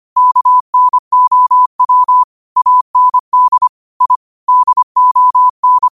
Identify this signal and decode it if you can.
Morse code, heard as a keyed tone, about 25 wpm, 1000 Hz. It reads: KNOW AND I DON